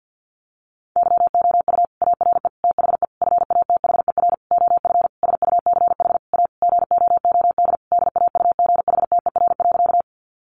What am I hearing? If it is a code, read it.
Morse code, heard as a keyed tone, about 50 wpm, 700 Hz. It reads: YOU ARE THE FATHER OF SUCH A GOOD DAUGHTER.
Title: Sentences from Top 1500 Words 50wpm